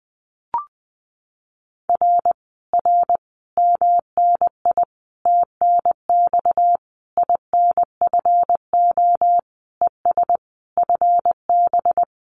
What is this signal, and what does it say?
Morse code, heard as a keyed tone, about 20 wpm, 700 Hz. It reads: R R MNI TNX INFO ES FB
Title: Common QSO Phrases Encore 20wpm